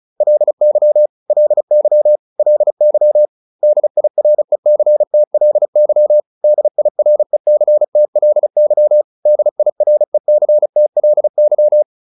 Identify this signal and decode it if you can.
Morse code, heard as a keyed tone, about 34 wpm, 600 Hz. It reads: LY LY LY DIRECTLY DIRECTLY DIRECTLY